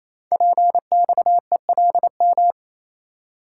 Morse code, heard as a keyed tone, about 28 wpm, 700 Hz. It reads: PXELM
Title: Code Groups - 5 Letters - 28wpm